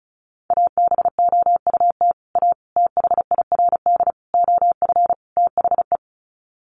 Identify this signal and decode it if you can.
Morse code, heard as a keyed tone, about 34 wpm, 700 Hz. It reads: ABOUT A THIRD OF THE
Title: Sets of 5 Words Encore2 35wpm